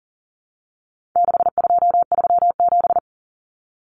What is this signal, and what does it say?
Morse code, heard as a keyed tone, about 40 wpm, 700 Hz. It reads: 6237
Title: Numbers - 4-Digit 40wpm